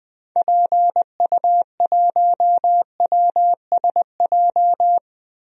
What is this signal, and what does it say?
Morse code, heard as a keyed tone, about 21 wpm, 700 Hz. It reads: PU1WSJ
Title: DXCC-100 Call Signs - 6 Characters - Repeated 3x - 20wpm